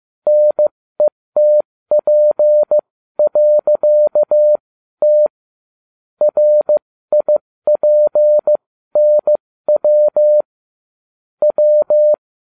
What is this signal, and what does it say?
Morse code, heard as a keyed tone, about 15 wpm, spaced as for 13 wpm, 600 Hz. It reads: NETP.T RIPNW W